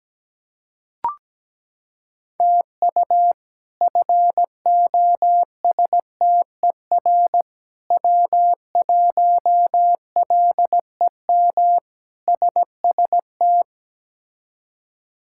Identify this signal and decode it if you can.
Morse code, heard as a keyed tone, about 17 wpm, 700 Hz. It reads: TU FOSTER W1LEM SST
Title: K1USN SST QSO Practice - 17wpm